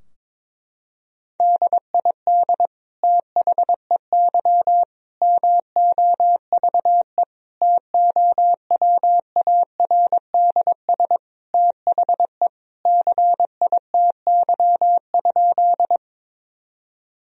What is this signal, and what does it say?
Morse code, heard as a keyed tone, about 21 wpm, 700 Hz. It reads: DID THEY MOVE TOWARDS THE CITY?
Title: Sentences from Top 1000 Words - Review Part 2 - 22wpm